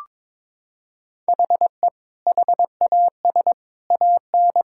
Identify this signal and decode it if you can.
Morse code, heard as a keyed tone, about 22 wpm, 700 Hz. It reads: HE HAS AN
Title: Sentences from Top 1000 Words - Review Part 6 - 22wpm